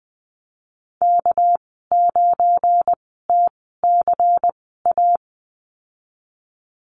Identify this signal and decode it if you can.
Morse code, heard as a keyed tone, about 20 wpm, spaced as for 15 wpm, 700 Hz. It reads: K9TCA
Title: 219 Lesson 35g - Number 0 - Call Signs 20wpm